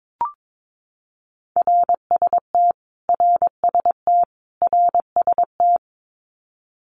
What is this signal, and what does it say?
Morse code, heard as a keyed tone, about 22 wpm, 700 Hz. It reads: RST RST RST
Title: Abbreviations Repeated 3x - 22wpm